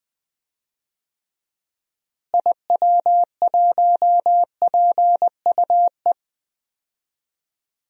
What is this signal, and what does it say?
Morse code, heard as a keyed tone, about 20 wpm, 700 Hz. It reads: IW1PUE
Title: DXCC-100 Call Signs - 6 Characters - Repeated 3x - 20wpm